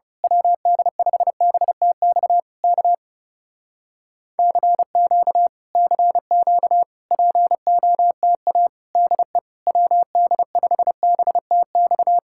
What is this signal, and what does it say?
Morse code, heard as a keyed tone, about 28 wpm, 700 Hz. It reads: WD5BTX K CQ CQ POTA DE WD5BTX
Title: Speed-Racing Common QSO Phrases - 30wpm